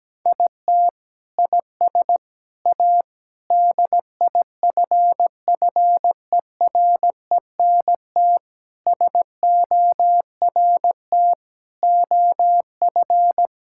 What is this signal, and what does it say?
Morse code, heard as a keyed tone, about 17 wpm, 700 Hz. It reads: IT IS A DIFFERENT SORT OF
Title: Sentences from Top 1000 Words - Review Part 5 - 17wpm